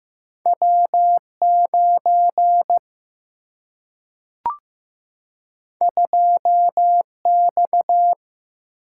Morse code, heard as a keyed tone, about 15 wpm, 700 Hz. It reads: W 9 2 X
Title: Single-Letter Single-Number 15wpm